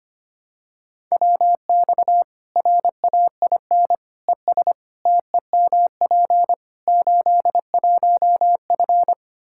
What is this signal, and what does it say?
Morse code, heard as a keyed tone, about 25 wpm, 700 Hz. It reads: WX RAIN ES TEMP 81F
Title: Practice QSO - Exchange 2 - Info - 25wpm